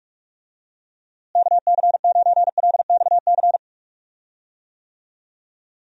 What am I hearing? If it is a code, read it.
Morse code, heard as a keyed tone, about 41 wpm, 700 Hz. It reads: KC9LXC
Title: USA Call Signs - 6 Characters - Repeated 2x - 45wpm